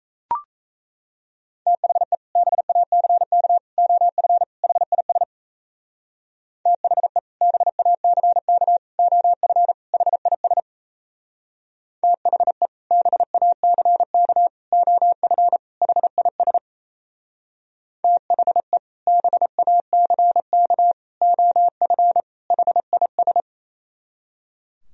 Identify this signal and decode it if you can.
Morse code, heard as a keyed tone, about 36 wpm, 700 Hz. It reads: THE BACK OF HIS THE BACK OF HIS THE BACK OF HIS THE BACK OF HIS